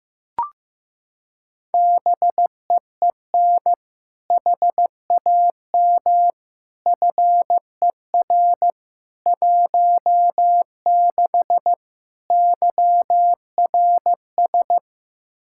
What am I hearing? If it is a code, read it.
Morse code, heard as a keyed tone, about 15 wpm, 700 Hz. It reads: BEEN HAM FER 16 YRS
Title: Practice QSO - Exchange 3 - More Info - 15wpm